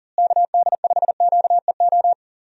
Morse code, heard as a keyed tone, about 40 wpm, 700 Hz. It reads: KD5QEO
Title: USA Call Signs - 6 Characters - Repeated 3x - 40wpm